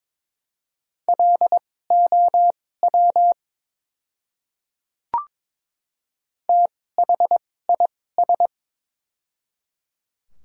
Morse code, heard as a keyed tone, about 22 wpm, spaced as for 16 wpm, 700 Hz. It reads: LOW THIS
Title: 2x Character Spacing - Top 300 Words - 22wpm